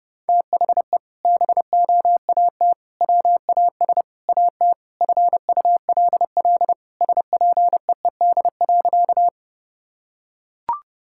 Morse code, heard as a keyed tone, about 30 wpm, 700 Hz. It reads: THE BOAT WAS AT FULL SPEED.
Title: Sentences from Top 1000 Words - Review Part 5 - 30wpm